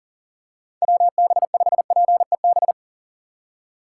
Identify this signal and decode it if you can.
Morse code, heard as a keyed tone, about 39 wpm, 700 Hz. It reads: WB5PEB
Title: USA Call Signs - 6 Characters - 4x Character Spacing - 40wpm